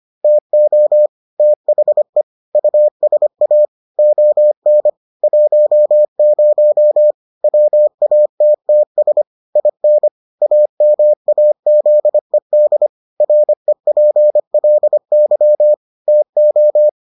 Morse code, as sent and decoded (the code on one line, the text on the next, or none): - --- / - .... / ..- ... .- / --- -. / .---- ----- / .-- .- - - ... / .. -. / .- -- .- --.. -.. / .-. .--. .-.. -.-- / - ---
TO THE USA ON 10 WATTS IN AMAZED REPLY TO